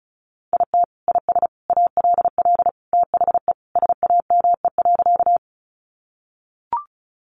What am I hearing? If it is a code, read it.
Morse code, heard as a keyed tone, about 34 wpm, 700 Hz. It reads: IT IS ALL THE SAME.
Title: Sentences from Top 200 Words 35wpm